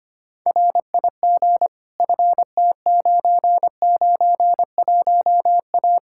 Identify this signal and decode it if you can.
Morse code, heard as a keyed tone, about 25 wpm, 700 Hz. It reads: RIG FT991A
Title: Practice QSO - Exchange 2 - Info - 25wpm